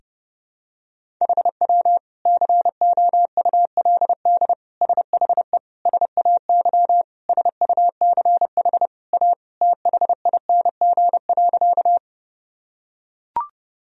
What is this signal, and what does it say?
Morse code, heard as a keyed tone, about 30 wpm, 700 Hz. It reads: HW COULD SHE SAY SUCH A THING.
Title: Sentences from Top 200 Words and Abbreviations 30wpm